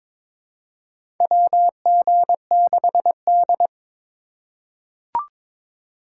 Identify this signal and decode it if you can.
Morse code, heard as a keyed tone, about 22 wpm, 700 Hz. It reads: WG6D
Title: USA Call Signs - 2x Character Spacing - 22wpm